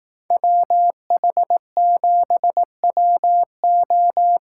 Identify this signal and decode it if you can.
Morse code, heard as a keyed tone, about 18 wpm, 700 Hz. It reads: WH7WO
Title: Speed Racing - USA 2x2 Call Signs 15wpm